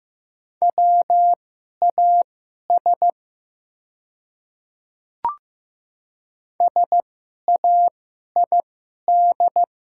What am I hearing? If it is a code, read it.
Morse code, heard as a keyed tone, about 15 wpm, spaced as for 11 wpm, 700 Hz. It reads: WAS SAID
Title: 2x Character Spacing - Top 100 Words - 15wpm